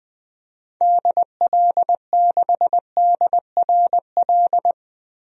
Morse code, heard as a keyed tone, about 20 wpm, 700 Hz. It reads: DL6DRL
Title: DXCC-100 Call Signs - 6 Characters - 20wpm